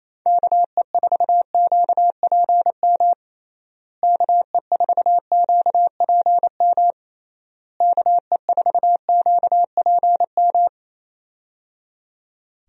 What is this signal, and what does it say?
Morse code, heard as a keyed tone, about 28 wpm, 700 Hz. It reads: KE4QPM KE4QPM KE4QPM
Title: USA Call Signs - 6 Characters - Repeated 3x - 28wpm